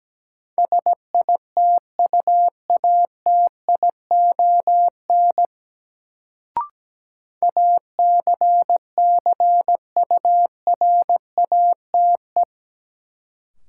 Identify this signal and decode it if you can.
Morse code, heard as a keyed tone, about 17 wpm, 700 Hz. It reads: SITUATION ACCURATE
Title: Sentences from Top 1200 Words - New Word Drill - 17wpm